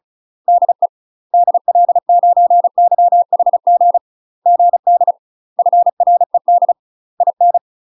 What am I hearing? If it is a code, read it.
Morse code, heard as a keyed tone, about 36 wpm, 700 Hz. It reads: DE DL9YHG GD FRED IN
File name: UK MASTER QSOs TRAINING / 35WPM / UK MASTER QSOs 0027